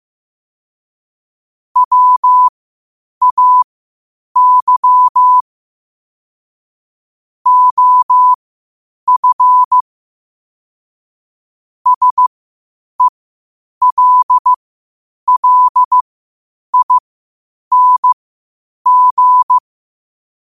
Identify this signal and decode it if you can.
Morse code, heard as a keyed tone, about 15 wpm, spaced as for 8 wpm, 1000 Hz. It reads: WAY OF SELLING
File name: qotdfiles / 8wpm